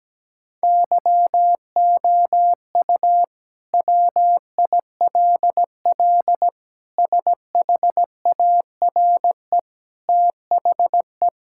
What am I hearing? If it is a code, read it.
Morse code, heard as a keyed tone, about 17 wpm, 700 Hz. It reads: YOU WILL SHARE THE
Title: Sentences from Top 1000 Words - Review Part 6 - 17wpm